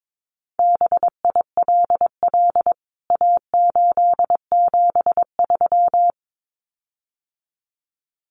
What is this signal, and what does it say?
Morse code, heard as a keyed tone, about 22 wpm, 700 Hz. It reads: BILL A873
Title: CWT Exchange 22wpm